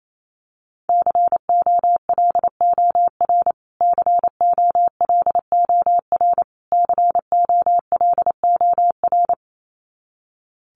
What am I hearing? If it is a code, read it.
Morse code, heard as a keyed tone, about 28 wpm, 700 Hz. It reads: COLOR COLOR COLOR
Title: Top 300 Words Repeated 28wpm